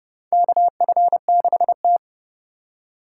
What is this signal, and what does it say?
Morse code, heard as a keyed tone, about 29 wpm, 700 Hz. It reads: KF6T
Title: K1USN SST QSO Practice - 30wpm